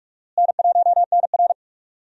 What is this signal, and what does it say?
Morse code, heard as a keyed tone, about 46 wpm, 700 Hz. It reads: N1NR
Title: USA Call Signs - 4 Characters - Repeated 2x - 45wpm